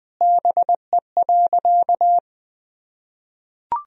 Morse code, heard as a keyed tone, about 20 wpm, 700 Hz. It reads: BE.
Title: Sentences from Top 300 Words and Abbreviations 20wpm